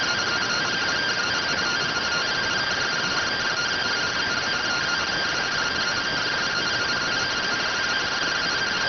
Two-tone teleprinter signal, about 170 Hz shift.